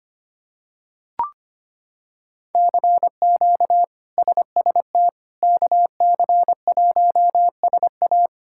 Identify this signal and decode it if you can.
Morse code, heard as a keyed tone, about 25 wpm, 700 Hz. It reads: CQ SST KC1SA
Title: K1USN SST QSO Practice - 25wpm